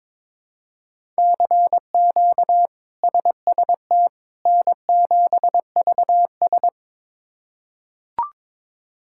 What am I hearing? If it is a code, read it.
Morse code, heard as a keyed tone, about 22 wpm, 700 Hz. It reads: CQ SST N7VS
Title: K1USN SST QSO Practice - 22wpm